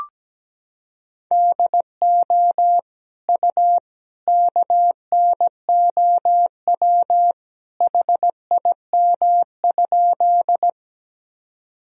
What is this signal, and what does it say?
Morse code, heard as a keyed tone, about 17 wpm, 700 Hz. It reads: DO U KNOW HIM?
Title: Sentences from Top 100 Words and Abbreviations 17wpm